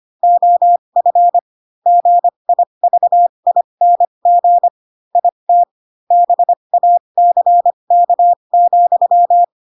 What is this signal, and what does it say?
Morse code, heard as a keyed tone, about 25 wpm, spaced as for 22 wpm, 700 Hz. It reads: OF GIVING IT BACK,
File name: Pinocchio / Pinocchio ch3 18 25 ik1hge